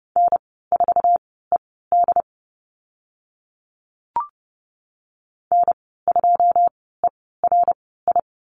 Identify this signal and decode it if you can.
Morse code, heard as a keyed tone, about 30 wpm, spaced as for 17 wpm, 700 Hz. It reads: N4ED N2ERI
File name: Lesson 14g - Number 2 - Call Signs 30wpm